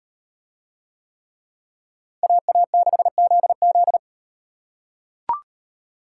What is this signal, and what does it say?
Morse code, heard as a keyed tone, about 36 wpm, 700 Hz. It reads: AA6ZZ